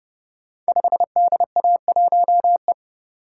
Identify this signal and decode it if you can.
Morse code, heard as a keyed tone, about 30 wpm, 700 Hz. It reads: 5DA1E
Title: Code Groups - 5 Letter:Numbers - 30wpm